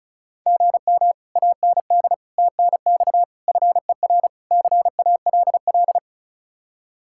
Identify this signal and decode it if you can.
Morse code, heard as a keyed tone, about 46 wpm, 700 Hz. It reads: GM AND TNX FER CALL